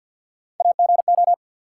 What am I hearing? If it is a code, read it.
Morse code, heard as a keyed tone, about 53 wpm, 700 Hz. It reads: AGO
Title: Sets of 2 Words - New Word Drill - 50wpm